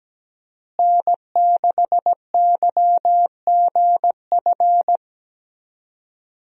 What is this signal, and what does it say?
Morse code, heard as a keyed tone, about 17 wpm, 700 Hz. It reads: N6YGF
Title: USA Call Signs - 5 Characters - Repeated 3x - 17wpm